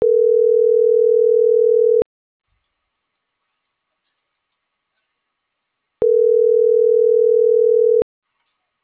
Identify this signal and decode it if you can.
Ringback tone (North America), 440 + 480 Hz.